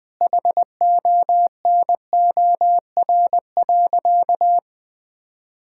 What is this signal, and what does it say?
Morse code, heard as a keyed tone, about 21 wpm, 700 Hz. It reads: HONOR.
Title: Sentences with 5-Word Phrases - 20wpm